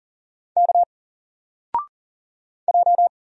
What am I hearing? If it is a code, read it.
Morse code, heard as a keyed tone, about 40 wpm, 700 Hz. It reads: K J
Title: Single Character (Letter:Number:Punctuation:Prosign) 40wpm